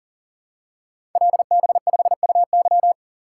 Morse code, heard as a keyed tone, about 42 wpm, 700 Hz. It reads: LB5UY
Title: Code Groups - 5 Letter:Numbers - 40wpm